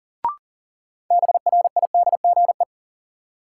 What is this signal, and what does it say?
Morse code, heard as a keyed tone, about 40 wpm, 700 Hz. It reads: BRIDGE
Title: Sentences from Top 1200 Words - New Word Drill Alt - 40wpm